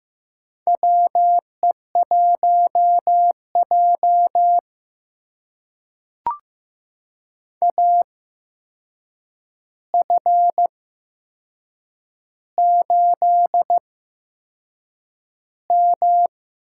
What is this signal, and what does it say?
Morse code, heard as a keyed tone, about 15 wpm, spaced as for 6 wpm, 700 Hz. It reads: WE1J AF8M
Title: USA Call Signs - 4 Characters - 8x Character Spacing - 15wpm